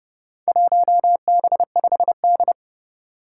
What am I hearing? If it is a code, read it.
Morse code, heard as a keyed tone, about 30 wpm, 700 Hz. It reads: 1B5D
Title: Four Challenging Letter-Number Combinations 30wpm